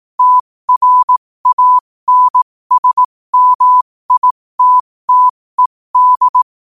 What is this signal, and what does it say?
Morse code, heard as a keyed tone, about 18 wpm, spaced as for 15 wpm, 1000 Hz. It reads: TRANSMITTED
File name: 15wpm